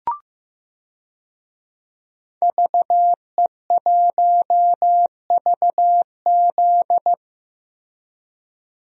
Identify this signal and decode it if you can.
Morse code, heard as a keyed tone, about 15 wpm, 700 Hz. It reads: VE1VZ
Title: DXCC-100 Call Signs - 5 Characters - Repeated 3x - 15wpm